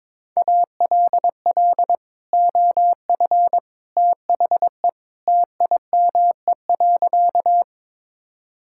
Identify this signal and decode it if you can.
Morse code, heard as a keyed tone, about 22 wpm, 700 Hz. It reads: ALL OF THE TIME.
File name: practice / 22wpm / Sentences with 3-Word Phrases - 22wpm